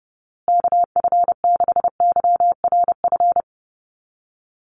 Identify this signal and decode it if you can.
Morse code, heard as a keyed tone, about 30 wpm, 700 Hz. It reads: KF6YRF
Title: USA Call Signs Repeated 2x 30wpm